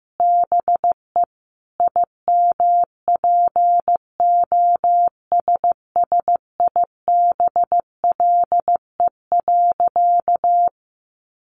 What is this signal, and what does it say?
Morse code, heard as a keyed tone, about 15 wpm, 700 Hz. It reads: BE IMPOSSIBLE.
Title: Sentences from Top 1100 Words 15wpm